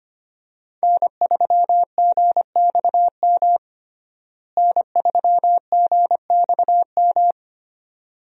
Morse code, heard as a keyed tone, about 25 wpm, 700 Hz. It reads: N3GXM N3GXM
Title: USA Call Signs - 5 Characters - Repeated 2x - 25wpm